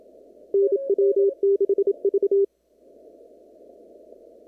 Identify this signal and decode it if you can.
Morse code, heard as a keyed tone, about 27 wpm, 400 Hz. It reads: NW6V
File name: CW / kt5j wpx cw-op-k5pi